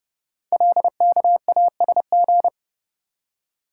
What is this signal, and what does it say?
Morse code, heard as a keyed tone, about 30 wpm, 700 Hz. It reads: LKASG
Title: Code Groups - 5 Letters - 30wpm